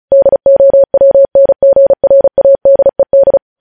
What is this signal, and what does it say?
Morse code, heard as a keyed tone, about 36 wpm, 550 Hz. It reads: DOWNGRADED
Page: news headlines in morse code at 35 wpm